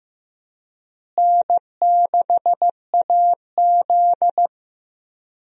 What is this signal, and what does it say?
Morse code, heard as a keyed tone, about 15 wpm, 700 Hz. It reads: N6AZ
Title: USA Call Signs - 4 Characters - Repeated 2x - 15wpm